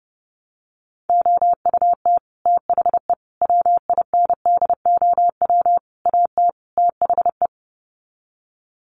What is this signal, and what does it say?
Morse code, heard as a keyed tone, about 30 wpm, 700 Hz. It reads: OUT THE WINDOW AT THE
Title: Sets of 5 Words Encore2 30wpm